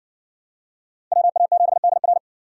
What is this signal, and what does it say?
Morse code, heard as a keyed tone, about 60 wpm, 700 Hz. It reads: WA7DR